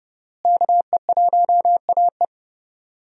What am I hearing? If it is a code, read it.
Morse code, heard as a keyed tone, about 30 wpm, 700 Hz. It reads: KE1AE
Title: USA Call Signs - 5 Characters - Repeated 2x - 30wpm